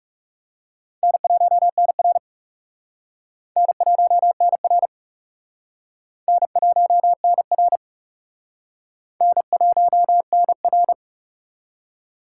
Morse code, heard as a keyed tone, about 38 wpm, 700 Hz. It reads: N1NR N1NR N1NR N1NR